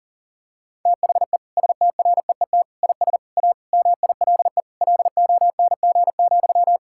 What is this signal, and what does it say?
Morse code, heard as a keyed tone, about 40 wpm, 700 Hz. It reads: THE STREET IS A MILE LONG,
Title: Sentences from Top 1600 Words 40wpm